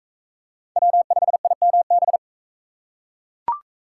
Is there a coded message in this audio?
Yes: Morse code from 0.8 to 2.2 s, heard as a keyed tone, about 40 wpm, 700 Hz. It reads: W5IMB